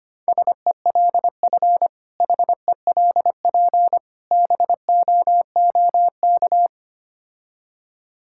Morse code, heard as a keyed tone, about 25 wpm, 700 Hz. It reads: SELF HELP BOOK
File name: practice / 25wpm / Difficult Words with Hints - 25wpm